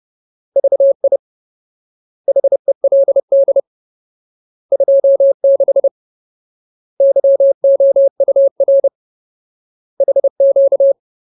Morse code, heard as a keyed tone, about 30 wpm, 550 Hz. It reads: VI HELD 26 YOUR HQ
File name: Ultimate Word List / 30WPM Ultimate